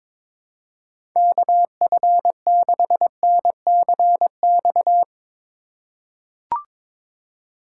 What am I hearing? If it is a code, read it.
Morse code, heard as a keyed tone, about 22 wpm, 700 Hz. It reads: KF6NCX
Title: K1USN SST QSO Practice - 22wpm